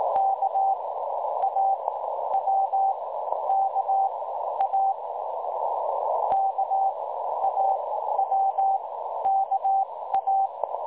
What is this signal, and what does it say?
Morse code, heard as a keyed tone, about 20 wpm, spaced as for 12 wpm, 780 Hz, with noise. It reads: KAWKA KAWKA
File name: dlazaawansowanych / nagraniazpasma / websdr recording 3565.0kHz